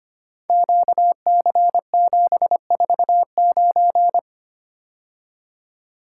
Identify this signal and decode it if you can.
Morse code, heard as a keyed tone, about 25 wpm, 700 Hz. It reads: QC749
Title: Code Groups - 5 Letter:Numbers - 25wpm